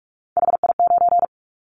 Morse code, heard as a keyed tone, about 46 wpm, 700 Hz. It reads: HI9
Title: Callsign Prefixes for Top 100 DXCC - 45wpm